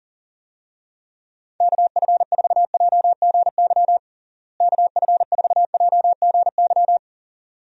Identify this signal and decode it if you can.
Morse code, heard as a keyed tone, about 40 wpm, 700 Hz. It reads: KF4JGY KF4JGY
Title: USA Call Signs - 6 Characters - Repeated 2x - 40wpm